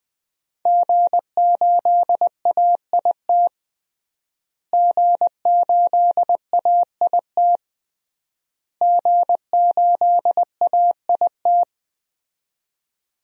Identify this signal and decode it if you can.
Morse code, heard as a keyed tone, about 20 wpm, 700 Hz. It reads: G8AIT G8AIT G8AIT
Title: DXCC-100 Call Signs - 5 Characters - Repeated 3x - 20wpm